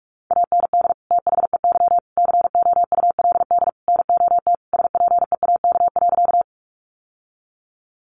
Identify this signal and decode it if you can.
Morse code, heard as a keyed tone, about 44 wpm, 700 Hz. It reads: AND THEY COULD NOT SPEAK.
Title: Sentences from Top 1100 Words 45wpm